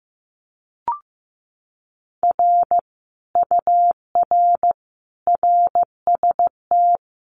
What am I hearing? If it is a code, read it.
Morse code, heard as a keyed tone, about 15 wpm, 700 Hz. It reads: R UR RST
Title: Common QSO Phrases Encore 15wpm